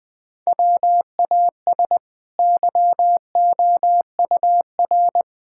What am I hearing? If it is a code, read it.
Morse code, heard as a keyed tone, about 20 wpm, 700 Hz. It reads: WAS YOUR
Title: Sentences from Top 1000 Words - Review Part 5 - 20wpm